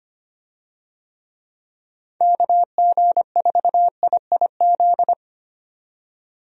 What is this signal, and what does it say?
Morse code, heard as a keyed tone, about 25 wpm, 700 Hz. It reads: KG4IIZ
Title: USA Call Signs - 6 Characters - 25wpm